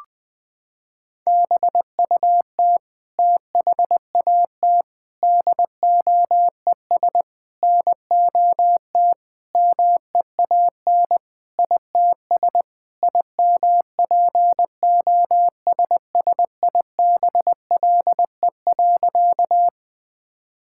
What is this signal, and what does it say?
Morse code, heard as a keyed tone, about 20 wpm, 700 Hz. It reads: BUT THAT DOES NOT MEAN ITS IMPOSSIBLE.
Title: Sentences with 5-Word Phrases - 20wpm